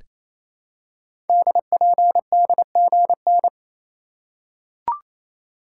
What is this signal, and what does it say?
Morse code, heard as a keyed tone, about 29 wpm, 700 Hz. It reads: DPDGN
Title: Code Groups - 5 Letters - 28wpm